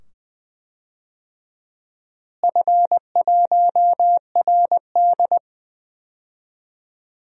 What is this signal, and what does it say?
Morse code, heard as a keyed tone, about 20 wpm, 700 Hz. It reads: F1RD
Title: DXCC-100 Call Signs - 4 Characters - Repeated 2x - 20wpm